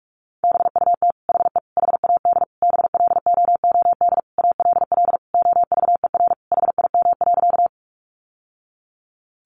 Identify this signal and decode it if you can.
Morse code, heard as a keyed tone, about 46 wpm, 700 Hz. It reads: BUT HE HAD BLOOD ALL OVER HIM.
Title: Sentences from Top 1100 Words 45wpm